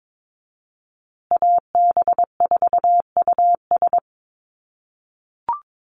Morse code, heard as a keyed tone, about 22 wpm, 700 Hz. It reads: AB4US
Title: USA Call Signs Repeated 2x 22wpm